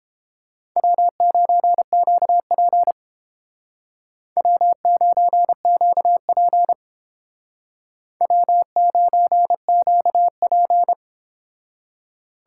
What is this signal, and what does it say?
Morse code, heard as a keyed tone, about 31 wpm, 700 Hz. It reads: W9QP W9QP W9QP